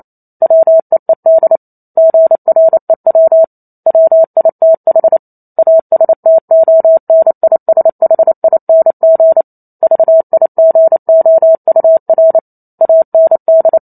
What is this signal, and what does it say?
Morse code, heard as a keyed tone, about 28 wpm, 650 Hz. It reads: WEED GREW WITH ASTONISHING VIGOUR AND